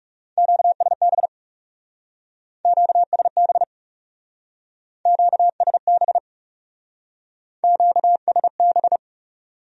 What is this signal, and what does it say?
Morse code, heard as a keyed tone, about 39 wpm, 700 Hz. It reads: QSB QSB QSB QSB